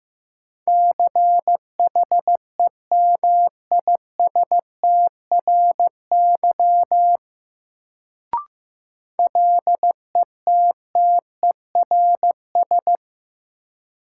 Morse code, heard as a keyed tone, about 15 wpm, 700 Hz. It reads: CHEMISTRY LETTERS
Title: Sentences from Top 1000 Words - Review Part 6 - New Word Drill - 15wpm